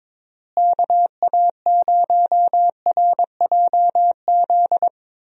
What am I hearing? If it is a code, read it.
Morse code, heard as a keyed tone, about 22 wpm, 700 Hz. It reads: KA0RJZ
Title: USA Call Signs - 6 Characters - Repeated 3x - 22wpm